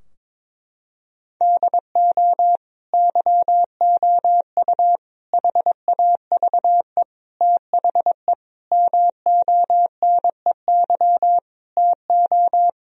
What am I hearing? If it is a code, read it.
Morse code, heard as a keyed tone, about 22 wpm, 700 Hz. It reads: DO YOU HAVE THE MONEY TO
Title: Sentences from Top 1000 Words - Review Part 2 - 22wpm